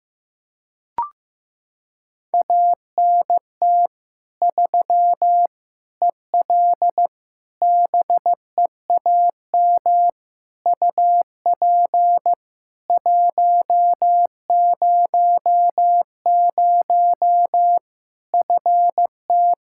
Morse code, heard as a keyed tone, about 15 wpm, 700 Hz. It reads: ANT 3 EL BEAM UP 100 FT
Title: Practice QSO - Exchange 2 - Info - 15wpm